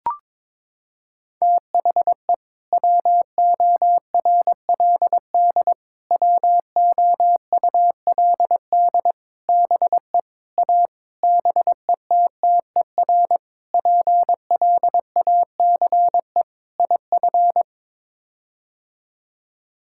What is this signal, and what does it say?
Morse code, heard as a keyed tone, about 22 wpm, 700 Hz. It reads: THE WORLD WOULD BE A BETTER PLACE IF
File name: practice / 22wpm / Sets of 8 Words - 22wpm